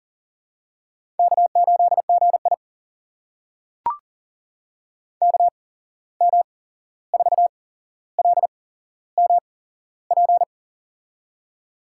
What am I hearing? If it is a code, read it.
Morse code, heard as a keyed tone, about 38 wpm, spaced as for 15 wpm, 700 Hz. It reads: K8GI KM4LMP
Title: USA Call Signs - 8x Character Spacing - 40wpm